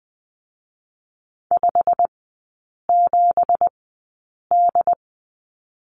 Morse code, heard as a keyed tone, about 20 wpm, spaced as for 8 wpm, 700 Hz. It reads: 57D
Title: 172 Lesson 28e - Number 9 - Three Characters 20wpm